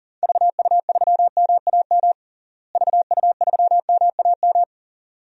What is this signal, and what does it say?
Morse code, heard as a keyed tone, about 40 wpm, 700 Hz. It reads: VU3MAM VU3MAM
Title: DXCC-100 Call Signs - 6 Characters - Repeated 2x - 40wpm